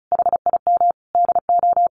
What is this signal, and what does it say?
Morse code, heard as a keyed tone, about 34 wpm, 700 Hz. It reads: HIM DO
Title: A Sound Of Thunder 35wpm